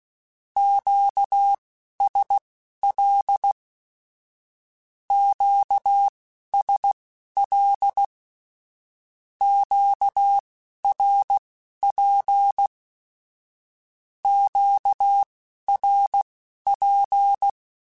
Morse code, heard as a keyed tone, about 16 wpm, spaced as for 11 wpm, 800 Hz. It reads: QSL QSL QRP QRP
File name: (spaced) / Q-Calls Spaced-mp3s / Q Calls Twice sp @15